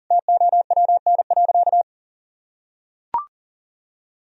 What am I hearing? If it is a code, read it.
Morse code, heard as a keyed tone, about 40 wpm, 700 Hz. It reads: TOWN.
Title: Sentences from Top 1000 Words - Review Part 4 - 40wpm